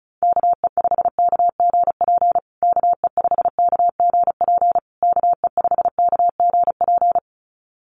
Morse code, heard as a keyed tone, about 34 wpm, 700 Hz. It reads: KE5KGP KE5KGP KE5KGP
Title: USA Call Signs Repeated 3x 35wpm